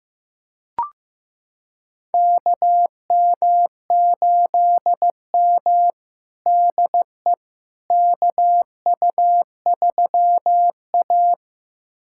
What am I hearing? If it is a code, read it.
Morse code, heard as a keyed tone, about 15 wpm, 700 Hz. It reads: KM8M DE KU3A
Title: USA Callsign DE Callsign - 4 Characters - 15wpm